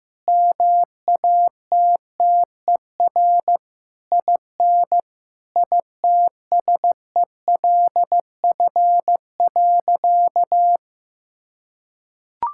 Morse code, heard as a keyed tone, about 15 wpm, 700 Hz. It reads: MATTER IN ITSELF.
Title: Sentences from Top 1000 Words - Review Part 5 - 15wpm